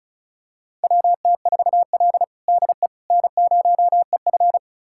Morse code, heard as a keyed tone, about 34 wpm, 700 Hz. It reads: WT4L DE N0EF
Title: USA Callsign DE Callsign - 4 Characters - 35wpm